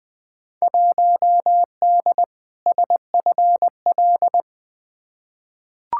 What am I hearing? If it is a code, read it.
Morse code, heard as a keyed tone, about 21 wpm, 700 Hz. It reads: 1D SFL
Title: Field Day Exchanges - 20wpm